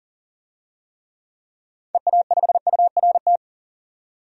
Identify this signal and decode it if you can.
Morse code, heard as a keyed tone, about 40 wpm, 700 Hz. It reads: EA5URT
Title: DXCC-100 Call Signs - 6 Characters - 40wpm